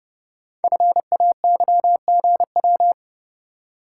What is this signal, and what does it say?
Morse code, heard as a keyed tone, about 30 wpm, 700 Hz. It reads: FAYGW
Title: Code Groups - 5 Letters - 30wpm